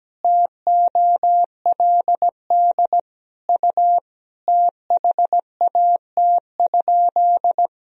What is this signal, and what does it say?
Morse code, heard as a keyed tone, about 17 wpm, 700 Hz. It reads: TOLD U THAT?
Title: Sentences from Top 400 Words and Abbreviations 17wpm